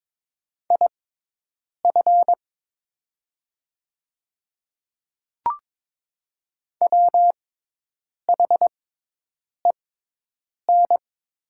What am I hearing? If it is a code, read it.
Morse code, heard as a keyed tone, about 22 wpm, spaced as for 8 wpm, 700 Hz. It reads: IF WHEN
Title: 6x Character Spacing - Top 100 Words - 22wpm